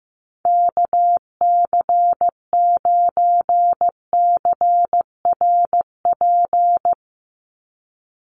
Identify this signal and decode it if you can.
Morse code, heard as a keyed tone, about 15 wpm, 700 Hz. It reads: KC9CRP
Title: USA Call Signs 15wpm